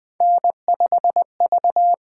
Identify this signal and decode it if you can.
Morse code, heard as a keyed tone, about 20 wpm, 700 Hz. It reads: N5V
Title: USA Call Signs - 4 Characters - 8x Character Spacing - 20wpm